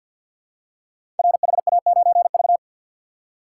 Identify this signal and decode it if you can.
Morse code, heard as a keyed tone, about 51 wpm, 700 Hz. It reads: RHA9V
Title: Code Groups - 5 Letter:Numbers - 50wpm